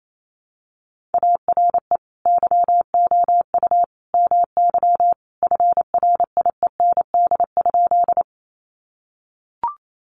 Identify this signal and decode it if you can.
Morse code, heard as a keyed tone, about 28 wpm, 700 Hz. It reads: ARE YOU MY FRIEND?
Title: Sentences from Top 300 Words 28wpm